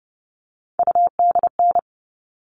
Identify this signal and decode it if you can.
Morse code, heard as a keyed tone, about 30 wpm, 700 Hz. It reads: UDN
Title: All Cut Numbers - 3-Digit 30wpm